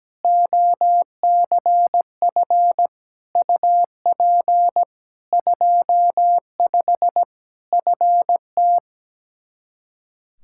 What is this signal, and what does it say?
Morse code, heard as a keyed tone, about 17 wpm, 700 Hz. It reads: OCF UP 25 FT
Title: Practice QSO - Exchange 2 - Info - 17wpm